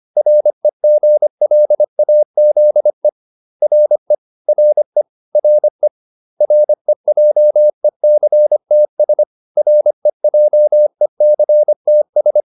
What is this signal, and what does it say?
Morse code, heard as a keyed tone, about 25 wpm, 600 Hz. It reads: REGLAZE RE RE RE REJECTS REJECTS